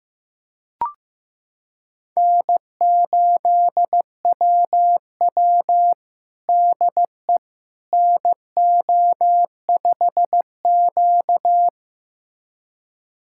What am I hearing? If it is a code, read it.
Morse code, heard as a keyed tone, about 15 wpm, 700 Hz. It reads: N8WW DE NO5Q
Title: USA Callsign DE Callsign - 4 Characters - 15wpm